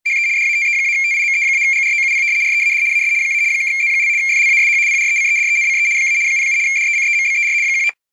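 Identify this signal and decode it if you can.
RTTY, about 75 baud. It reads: HE DATE-TIME-GROUP READ AS - (Z) FLASH PRECEDENCE - 07 DECEMBER 19
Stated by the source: Teletype on-air data stream sound effect.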